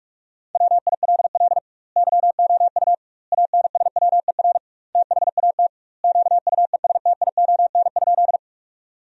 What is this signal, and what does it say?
Morse code, heard as a keyed tone, about 46 wpm, 700 Hz. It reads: WILL YOU ANSWER THAT QUESTION?
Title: 2x Word-Spacing - Sentences from Top 400 Words - 45wpm